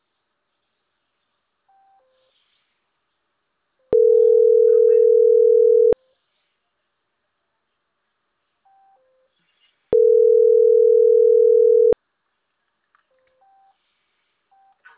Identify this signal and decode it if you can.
Ringback tone (North America), 440 + 480 Hz.